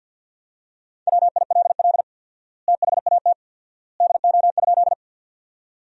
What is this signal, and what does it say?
Morse code, heard as a keyed tone, about 50 wpm, 700 Hz. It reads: WILL THAT DO?
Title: 4x Word-Spacing - Sentences from Top 100 Words - 50wpm